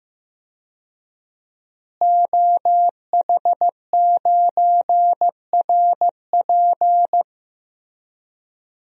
Morse code, heard as a keyed tone, about 15 wpm, 700 Hz. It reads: OH9RP
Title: DXCC-100 Call Signs - 5 Characters - Repeated 2x - 15wpm